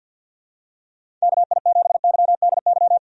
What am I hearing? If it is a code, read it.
Morse code, heard as a keyed tone, about 50 wpm, 700 Hz. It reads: KI7YDY
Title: USA Call Signs - 6 Characters - 50wpm